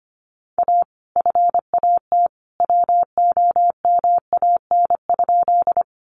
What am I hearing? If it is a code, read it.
Morse code, heard as a keyed tone, about 25 wpm, 700 Hz. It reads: A FAT WOMAN?
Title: Sentences from Top 1100 Words 25wpm